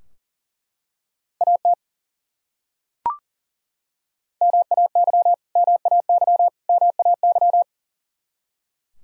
Morse code, heard as a keyed tone, about 40 wpm, 700 Hz. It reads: AT MAY MAY MAY
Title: Top 100 Words Repeated 40wpm